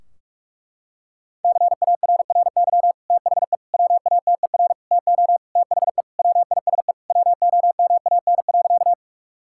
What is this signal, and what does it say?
Morse code, heard as a keyed tone, about 47 wpm, 700 Hz. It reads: CARRY THE WATER TO THE WISE WOMAN.
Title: Sentences from Top 1600 Words 45wpm